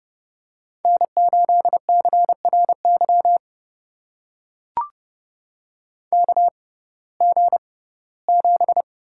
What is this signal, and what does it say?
Morse code, heard as a keyed tone, about 30 wpm, spaced as for 18 wpm, 700 Hz. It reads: N8CRY KG7
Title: USA Call Signs - 6x Character Spacing - 30wpm